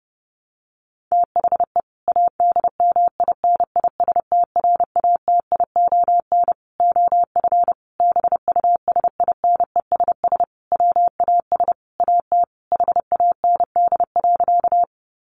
Morse code, heard as a keyed tone, about 30 wpm, 700 Hz. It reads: THE ADMINISTRATION OF BUSINESS WAS AT HAND.
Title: Sentences from Top 1300 Words 30wpm